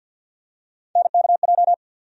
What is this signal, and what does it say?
Morse code, heard as a keyed tone, about 52 wpm, 700 Hz. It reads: NKJ
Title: USA Call Sign Postfixes - 3 Letters 50wpm